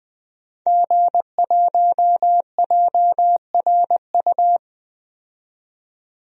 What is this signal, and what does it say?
Morse code, heard as a keyed tone, about 20 wpm, 700 Hz. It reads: G1JRU
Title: DXCC-100 Call Signs - 5 Characters - 20wpm